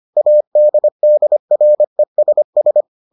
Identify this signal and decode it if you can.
Morse code, heard as a keyed tone, about 25 wpm, 600 Hz. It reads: ADDRESS